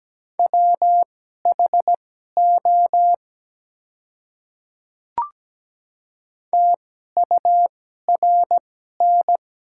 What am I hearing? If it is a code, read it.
Morse code, heard as a keyed tone, about 17 wpm, spaced as for 12 wpm, 700 Hz. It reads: WHO TURN
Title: 2x Character Spacing - Top 200 Words - 17wpm